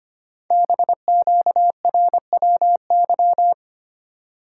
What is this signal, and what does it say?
Morse code, heard as a keyed tone, about 25 wpm, 700 Hz. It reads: BQRWY
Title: Code Groups - 5 Letters - 25wpm